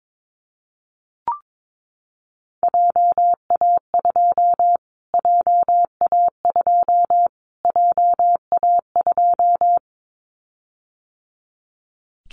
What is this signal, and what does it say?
Morse code, heard as a keyed tone, about 22 wpm, 700 Hz. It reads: JA2 JA2 JA2
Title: Callsign Prefixes for Top 25 DXCC - Repeated 3x - 22wpm